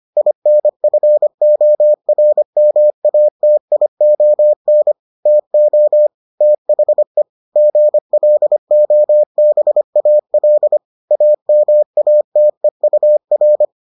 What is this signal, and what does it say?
Morse code, heard as a keyed tone, about 25 wpm, 600 Hz. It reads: INFORMATION TO THE GLOBAL AMATEUR